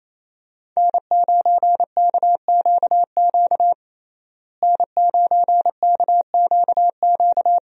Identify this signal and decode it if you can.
Morse code, heard as a keyed tone, about 27 wpm, 700 Hz. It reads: N9KQQ N9KQQ
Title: USA Call Signs - 5 Characters - Repeated 2x - 28wpm